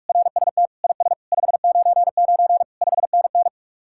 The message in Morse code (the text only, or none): RST IS 599 5NN